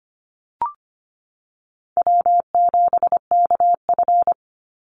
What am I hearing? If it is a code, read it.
Morse code, heard as a keyed tone, about 25 wpm, 700 Hz. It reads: W7KF
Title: Easy Call Signs Repeated 3x - 25wpm